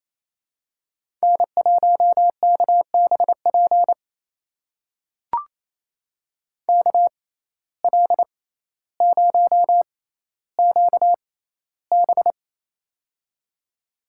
Morse code, heard as a keyed tone, about 27 wpm, spaced as for 14 wpm, 700 Hz. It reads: N1KBP KL0QB
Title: USA Call Signs - 5 Characters - 6x Character Spacing - 28wpm